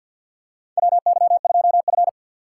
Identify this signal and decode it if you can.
Morse code, heard as a keyed tone, about 47 wpm, 700 Hz. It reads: WY2F
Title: USA Call Signs - 6x Character Spacing - 50wpm